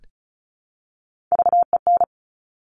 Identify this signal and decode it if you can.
Morse code, heard as a keyed tone, about 36 wpm, 700 Hz. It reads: VEN
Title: Top Three Letter Combinations in Words 35wpm